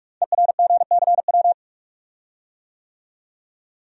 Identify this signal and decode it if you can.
Morse code, heard as a keyed tone, about 41 wpm, 700 Hz. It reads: ERGCW
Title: Code Groups - 5 Letter:Numbers - 45wpm